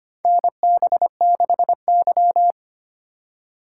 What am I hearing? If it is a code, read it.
Morse code, heard as a keyed tone, about 25 wpm, 700 Hz. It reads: NB6Y
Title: USA Call Signs - 4 Characters - 2x Character Spacing - 25wpm